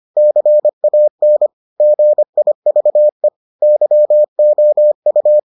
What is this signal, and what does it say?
Morse code, heard as a keyed tone, about 25 wpm, 600 Hz. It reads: CAN GIVE YOU